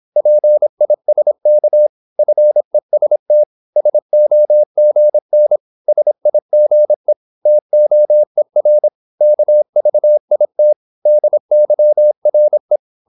Morse code, heard as a keyed tone, about 26 wpm, 600 Hz. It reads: PISK FEST SOGN SIGE TOER KVIT DYRE
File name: Korte ord DK 26wpm